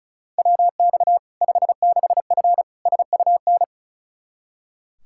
Morse code, heard as a keyed tone, about 36 wpm, 700 Hz. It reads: WX 56F SUN
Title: Practice QSO - Exchange 2 - Info - 35wpm